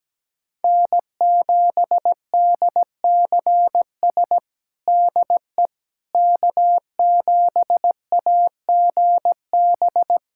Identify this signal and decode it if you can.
Morse code, heard as a keyed tone, about 17 wpm, 700 Hz. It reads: N7DCS DE K7AGB
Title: USA Callsign DE Callsign 17wpm